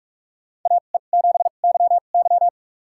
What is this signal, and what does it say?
Morse code, heard as a keyed tone, about 44 wpm, spaced as for 32 wpm, 700 Hz. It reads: AE7YY
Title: USA Call Signs - 5 Characters - 2x Character Spacing - 45wpm